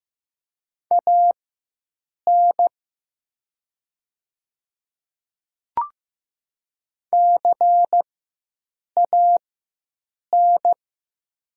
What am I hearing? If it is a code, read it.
Morse code, heard as a keyed tone, about 15 wpm, spaced as for 7 wpm, 700 Hz. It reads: AN CAN
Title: 4x Character Spacing - Top 100 Words - 15wpm